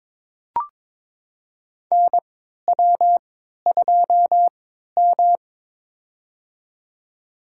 Morse code, heard as a keyed tone, about 21 wpm, spaced as for 12 wpm, 700 Hz. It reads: NW2M
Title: USA Call Signs - 4 Characters - 3x Character Spacing - 22wpm